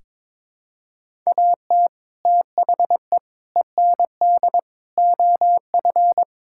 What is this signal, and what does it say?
Morse code, heard as a keyed tone, about 21 wpm, 700 Hz. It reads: AT THE END OF
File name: practice / 22wpm / Sentences with 4-Word Phrases - 22wpm